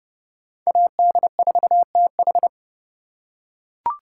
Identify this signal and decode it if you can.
Morse code, heard as a keyed tone, about 29 wpm, 700 Hz. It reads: AD4TH